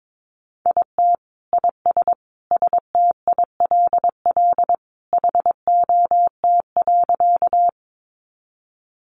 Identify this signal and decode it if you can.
Morse code, heard as a keyed tone, about 22 wpm, 700 Hz. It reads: IT IS STILL HOT.
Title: Additional Easy Phrases 22wpm